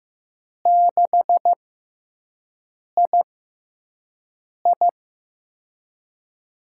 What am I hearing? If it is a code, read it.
Morse code, heard as a keyed tone, about 15 wpm, spaced as for 5 wpm, 700 Hz. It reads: 6II
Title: USA Call Signs - 4 Characters - 6x Character Spacing - 15wpm